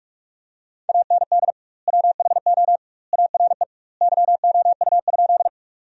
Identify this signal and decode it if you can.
Morse code, heard as a keyed tone, about 47 wpm, 700 Hz. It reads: AND WHO ARE YOU?
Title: 2x Word-Spacing - Sentences from Top 100 Words - 45wpm